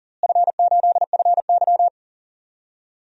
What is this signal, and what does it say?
Morse code, heard as a keyed tone, about 40 wpm, 700 Hz. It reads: F8FY
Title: DXCC-100 Call Signs - 4 Characters - Repeated 2x - 40wpm